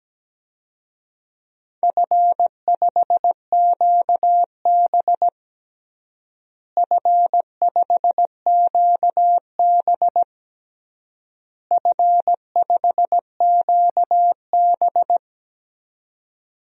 Morse code, heard as a keyed tone, about 17 wpm, 700 Hz. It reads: F5QB F5QB F5QB
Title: DXCC-100 Call Signs - 4 Characters - Repeated 3x - 17wpm